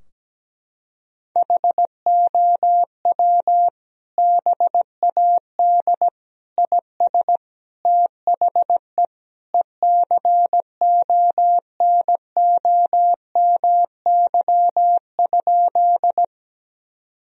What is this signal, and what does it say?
Morse code, heard as a keyed tone, about 17 wpm, 700 Hz. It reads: HOW BAD IS THE ECONOMY?
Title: Sentences from Top 1000 Words - Review Part 3 - 17wpm